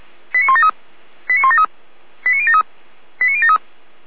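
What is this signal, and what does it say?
Five-tone selective call (ZVEI), heard as quick runs of five tones, tones of about 70 ms. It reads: 89173 89173 89073 89073
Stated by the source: ZVEI five-tone selective call received via SDR.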